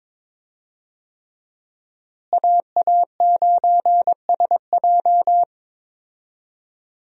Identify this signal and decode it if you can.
Morse code, heard as a keyed tone, about 22 wpm, 700 Hz. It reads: AA9SJ
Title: USA Call Signs - 5 Characters - Repeated 3x - 22wpm